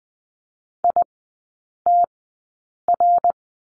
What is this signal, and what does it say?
Morse code, heard as a keyed tone, about 20 wpm, spaced as for 8 wpm, 700 Hz. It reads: ITR
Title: 060 Lesson 11e - Letter h - Three Characters 20wpm